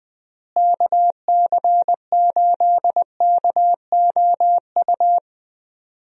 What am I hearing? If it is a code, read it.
Morse code, heard as a keyed tone, about 20 wpm, 700 Hz. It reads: KC8KOU
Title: USA Call Signs - 6 Characters - Repeated 3x - 20wpm